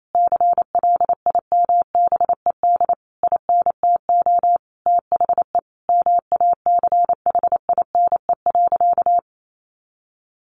Morse code, heard as a keyed tone, about 28 wpm, 700 Hz. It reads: CLIMBED INTO THE MACHINE.
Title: A Sound Of Thunder 28wpm